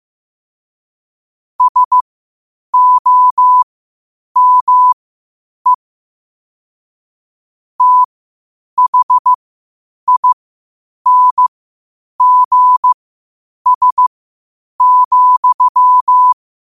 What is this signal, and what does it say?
Morse code, heard as a keyed tone, about 15 wpm, spaced as for 8 wpm, 1000 Hz. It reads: SOME THINGS,
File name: qotdfiles / 8wpm